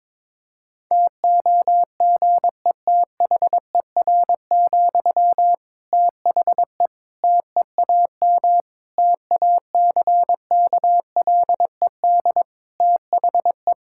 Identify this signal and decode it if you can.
Morse code, heard as a keyed tone, about 22 wpm, 700 Hz. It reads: TOGETHER, THE TEAM TACKLED THE
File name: Difficult Sentences with Hints - 22wpm